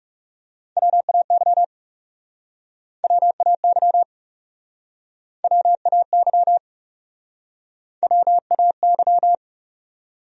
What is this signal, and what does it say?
Morse code, heard as a keyed tone, about 42 wpm, 700 Hz. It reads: WAY WAY WAY WAY